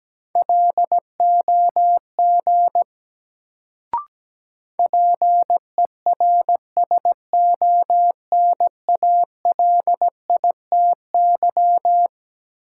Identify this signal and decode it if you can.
Morse code, heard as a keyed tone, about 17 wpm, 700 Hz. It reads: LOG PERSONALITY
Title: Sentences from Top 1400 Words - New Word Drill - 17wpm